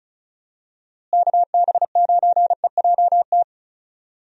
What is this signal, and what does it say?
Morse code, heard as a keyed tone, about 34 wpm, 700 Hz. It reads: KB9EJT
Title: USA Call Signs - 6 Characters - Repeated 3x - 35wpm